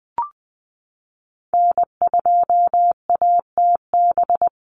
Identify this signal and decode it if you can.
Morse code, heard as a keyed tone, about 20 wpm, 700 Hz. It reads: N2ATB
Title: Call Signs Repeated 3x - 20wpm